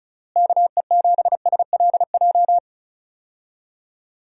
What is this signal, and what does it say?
Morse code, heard as a keyed tone, about 36 wpm, 700 Hz. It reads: KE7SLJ
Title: USA Call Signs - 6 Characters - Repeated 3x - 35wpm